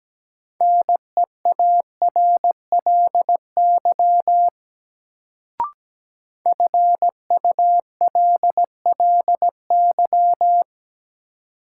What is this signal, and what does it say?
Morse code, heard as a keyed tone, about 17 wpm, 700 Hz. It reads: NEARLY FULLY
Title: Sentences from Top 1300 Words - New Word Drill - 17wpm